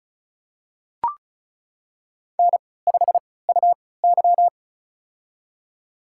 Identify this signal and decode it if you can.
Morse code, heard as a keyed tone, about 36 wpm, spaced as for 20 wpm, 700 Hz. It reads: N5UY
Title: USA Call Signs - 4 Characters - 3x Character Spacing - 35wpm